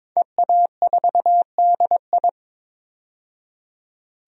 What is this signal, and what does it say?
Morse code, heard as a keyed tone, about 22 wpm, 700 Hz. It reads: EA4DI
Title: DXCC-100 Call Signs - 5 Characters - 22wpm